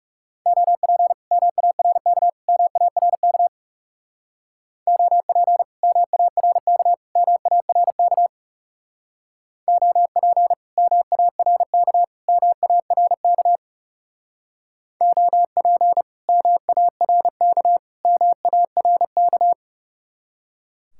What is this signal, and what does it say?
Morse code, heard as a keyed tone, about 39 wpm, 700 Hz. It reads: OP MARK MARK OP MARK MARK OP MARK MARK OP MARK MARK